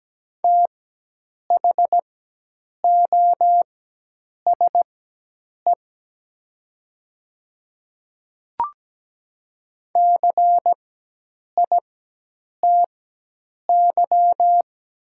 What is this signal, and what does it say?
Morse code, heard as a keyed tone, about 17 wpm, spaced as for 8 wpm, 700 Hz. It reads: THOSE CITY
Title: 4x Character Spacing - Top 300 Words - 17wpm